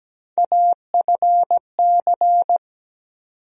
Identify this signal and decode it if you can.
Morse code, heard as a keyed tone, about 17 wpm, 700 Hz. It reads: AFC
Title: USA Call Sign Postfixes - 3 Letters 17wpm